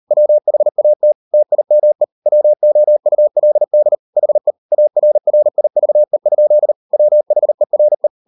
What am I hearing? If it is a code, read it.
Morse code, heard as a keyed tone, about 40 wpm, 600 Hz. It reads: WHAT TIME WOULD HE ARRIVE? WHERE